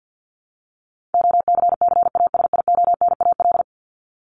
Morse code, heard as a keyed tone, about 50 wpm, 700 Hz. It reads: OCCASIONAL
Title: LY Suffix - Top 100 Words with Base Words 50wpm